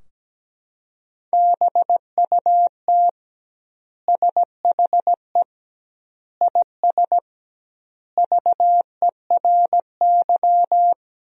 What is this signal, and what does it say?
Morse code, heard as a keyed tone, about 17 wpm, 700 Hz. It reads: BUT SHE IS VERY
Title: 2x Word-Spacing - Sentences from Top 400 Words - 17wpm